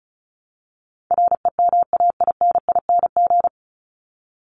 Morse code, heard as a keyed tone, about 34 wpm, 700 Hz. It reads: REMAINING
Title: ING Suffix - Top 100 Words with Base Words 35wpm